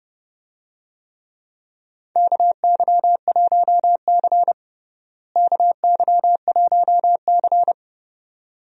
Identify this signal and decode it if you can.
Morse code, heard as a keyed tone, about 30 wpm, 700 Hz. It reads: KY1C KY1C
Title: USA Call Signs - 4 Characters - Repeated 2x - 30wpm